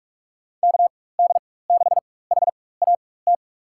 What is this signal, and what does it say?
Morse code, heard as a keyed tone, about 45 wpm, spaced as for 21 wpm, 700 Hz. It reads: KD6HAT
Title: USA Call Signs - 4x Character Spacing - 45wpm